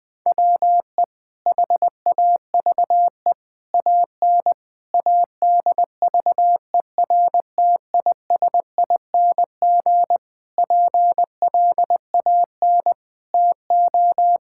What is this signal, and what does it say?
Morse code, heard as a keyed tone, about 20 wpm, 700 Hz. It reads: WE HAVE AN ADVERTISING PLAN TO
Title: Sentences from Top 1000 Words - Review Part 6 - 20wpm